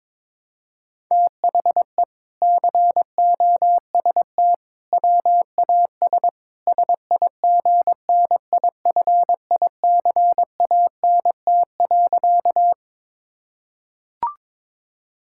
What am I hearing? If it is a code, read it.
Morse code, heard as a keyed tone, about 22 wpm, 700 Hz. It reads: THE COST WAS SIGNIFICANT.
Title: Sentences from Top 1000 Words - Review Part 6 - 22wpm